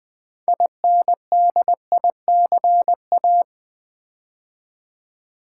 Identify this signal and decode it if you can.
Morse code, heard as a keyed tone, about 20 wpm, 700 Hz. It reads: INDICA
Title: Build A Word - 3 Letters At A Time - 9-Letter Words - 20wpm